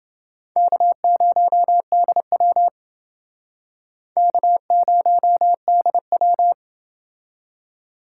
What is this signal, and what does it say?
Morse code, heard as a keyed tone, about 29 wpm, 700 Hz. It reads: K0DW K0DW